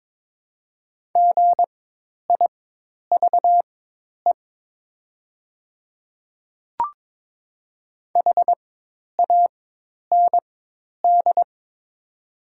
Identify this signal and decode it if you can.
Morse code, heard as a keyed tone, about 22 wpm, spaced as for 10 wpm, 700 Hz. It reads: GIVE HAND
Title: 4x Character Spacing - Top 200 Words - 22wpm